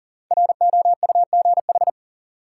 Morse code, heard as a keyed tone, about 40 wpm, 700 Hz. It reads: ROUGH
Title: Five Letter Words - 40wpm